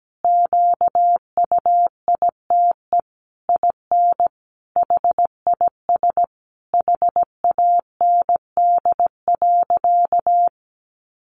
Morse code, heard as a keyed tone, about 17 wpm, 700 Hz. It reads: QUITE IN HIS HAND.
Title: Sentences from Top 1400 Words 17wpm